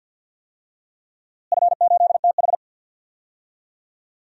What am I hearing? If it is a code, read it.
Morse code, heard as a keyed tone, about 52 wpm, 700 Hz. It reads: F8TH
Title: DXCC-100 Call Signs - 4 Characters - Repeated 2x - 50wpm